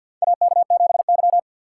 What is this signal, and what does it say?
Morse code, heard as a keyed tone, about 52 wpm, 700 Hz. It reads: AK7Y
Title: USA Call Signs - 3x Character Spacing - 50wpm